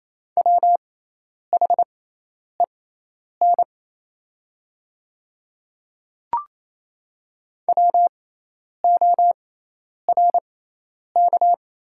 Morse code, heard as a keyed tone, about 27 wpm, spaced as for 10 wpm, 700 Hz. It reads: WHEN WORK
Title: 6x Character Spacing - Top 300 Words - 28wpm